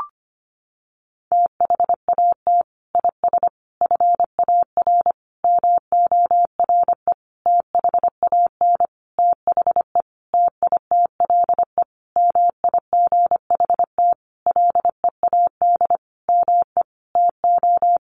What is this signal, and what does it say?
Morse code, heard as a keyed tone, about 25 wpm, 700 Hz. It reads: THAT IS FAR MORE THAN THE TITLE MIGHT LEAD ME TO
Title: Sentences from Top 1200 Words 25wpm